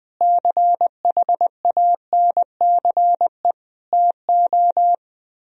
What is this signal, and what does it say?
Morse code, heard as a keyed tone, about 20 wpm, 700 Hz. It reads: CHANCE TO
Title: Speed-Racing - Sets of 4 Words 20wpm